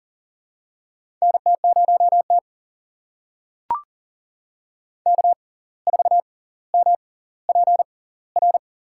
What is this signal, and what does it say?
Morse code, heard as a keyed tone, about 40 wpm, spaced as for 19 wpm, 700 Hz. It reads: NT0T K4MPR
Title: USA Call Signs - 6x Character Spacing - 40wpm